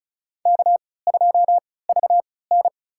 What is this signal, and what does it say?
Morse code, heard as a keyed tone, about 36 wpm, spaced as for 20 wpm, 700 Hz. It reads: K2VN
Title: USA Call Signs - 4 Characters - 3x Character Spacing - 35wpm